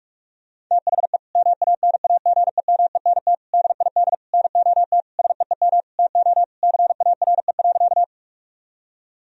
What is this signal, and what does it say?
Morse code, heard as a keyed tone, about 46 wpm, 700 Hz. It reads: THE MANAGEMENT DID NOT SEEM TO CARE.
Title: Sentences from Top 1000 Words - Review Part 4 - 45wpm